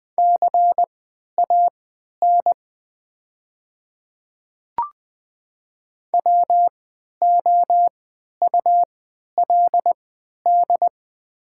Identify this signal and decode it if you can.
Morse code, heard as a keyed tone, about 20 wpm, spaced as for 11 wpm, 700 Hz. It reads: CAN WOULD
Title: 3x Character Spacing - Top 100 Words - 20wpm